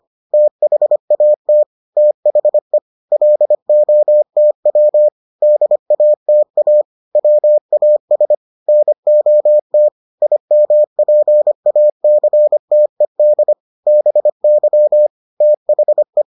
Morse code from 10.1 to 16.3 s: IMPACTED BY THE

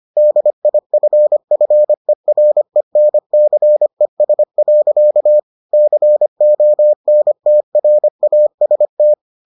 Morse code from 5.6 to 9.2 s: CONTRAST

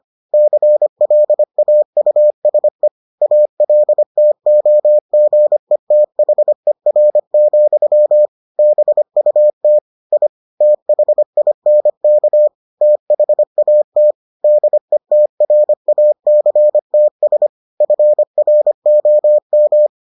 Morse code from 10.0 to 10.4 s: I

